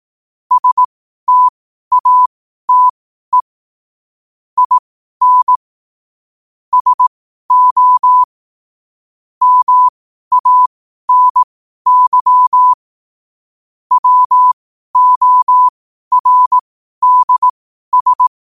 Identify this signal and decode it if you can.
Morse code, heard as a keyed tone, about 18 wpm, spaced as for 12 wpm, 1000 Hz. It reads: STATE IN SO MANY WORDS
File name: qotdfiles / 12wpm